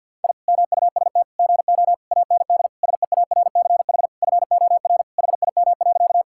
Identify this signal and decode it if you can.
Morse code, heard as a keyed tone, about 50 wpm, 700 Hz. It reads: I MUST GO AND SEARCH FOR HIM.
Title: Sentences from Top 1000 Words - Review Part 5 - 50wpm